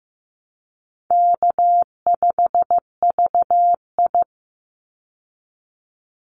Morse code, heard as a keyed tone, about 15 wpm, 700 Hz. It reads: K5VI
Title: Easy Call Signs Repeated 3x - 15wpm